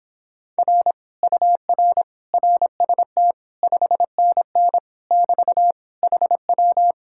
Morse code, heard as a keyed tone, about 26 wpm, 700 Hz. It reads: R UR RST 5NN <BT> HW